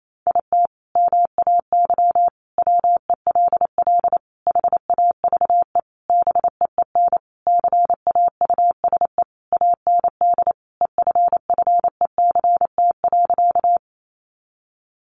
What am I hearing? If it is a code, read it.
Morse code, heard as a keyed tone, about 28 wpm, 700 Hz. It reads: IT MAY WELL HAVE BEEN CAUSE AND EFFECT.
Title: Sentences from Top 1000 Words 28wpm